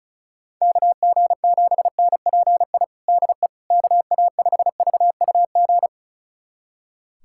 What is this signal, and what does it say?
Morse code, heard as a keyed tone, about 34 wpm, 700 Hz. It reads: KG7NPI DE KA5VUG
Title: USA Callsign DE Callsign - 6 Characters - 35wpm